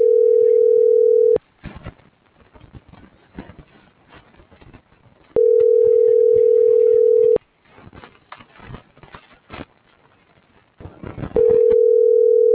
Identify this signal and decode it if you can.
Ringback tone (North America), 440 + 480 Hz.